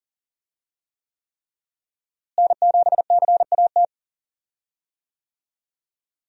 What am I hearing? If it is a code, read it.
Morse code, heard as a keyed tone, about 39 wpm, 700 Hz. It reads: N7CAT
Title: USA Call Signs - 5 Characters - Repeated 3x - 40wpm